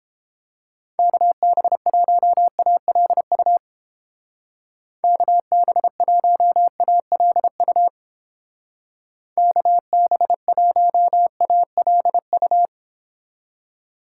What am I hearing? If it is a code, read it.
Morse code, heard as a keyed tone, about 29 wpm, 700 Hz. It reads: KB1ALU KB1ALU KB1ALU